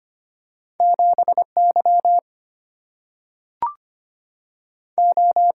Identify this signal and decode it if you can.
Morse code, heard as a keyed tone, about 25 wpm, 700 Hz. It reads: 7Y O
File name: Two Letter-Number Combinations - 25wpm